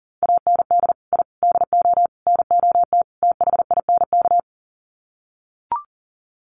Morse code, heard as a keyed tone, about 40 wpm, 700 Hz. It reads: AND I DO NOT THINK
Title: Sets of 5 Words 40wpm